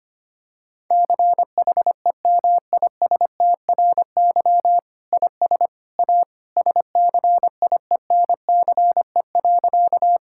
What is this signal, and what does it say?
Morse code, heard as a keyed tone, about 25 wpm, 700 Hz. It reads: CHEMISTRY IS A SCIENCE.
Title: Sentences from Top 1000 Words - Review Part 6 - 25wpm